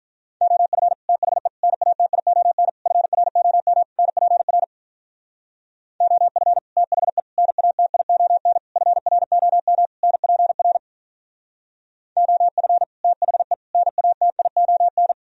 Morse code, heard as a keyed tone, about 46 wpm, 700 Hz. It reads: OF THE NATION FROM NPR OF THE NATION FROM NPR OF THE NATION